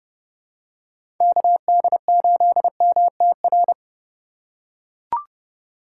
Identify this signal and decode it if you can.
Morse code, heard as a keyed tone, about 30 wpm, 700 Hz. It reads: KD8MTR
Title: USA Call Signs - 2x Character Spacing - 30wpm